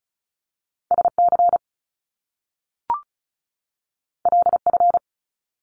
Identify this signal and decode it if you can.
Morse code, heard as a keyed tone, about 36 wpm, 700 Hz. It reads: SC LF
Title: Top Two Letter Combinations in Words 35wpm